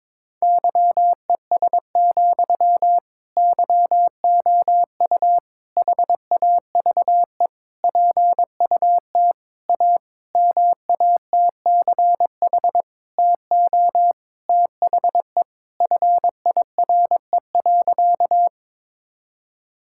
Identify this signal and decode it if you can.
Morse code, heard as a keyed tone, about 21 wpm, 700 Hz. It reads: YES, YOU HAVE PUT A MATCH TO THE FIRE.
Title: Sentences from Top 1600 Words 22wpm